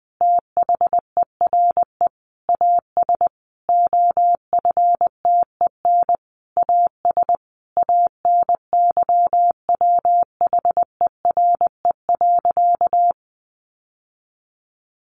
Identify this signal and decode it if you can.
Morse code, heard as a keyed tone, about 20 wpm, 700 Hz. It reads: THERE AS OFTEN AS ANYWHERE.
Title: Sentences from Top 1400 Words 20wpm